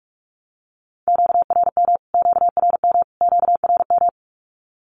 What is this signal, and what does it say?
Morse code, heard as a keyed tone, about 44 wpm, 700 Hz. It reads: QRM QRM QRM
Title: Q-Codes Repeated 3x 45wpm